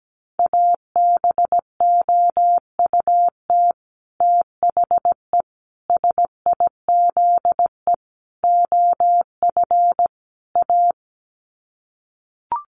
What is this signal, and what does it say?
Morse code, heard as a keyed tone, about 17 wpm, 700 Hz. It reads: ABOUT THE SIZE OF A
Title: Sets of 5 Words 17wpm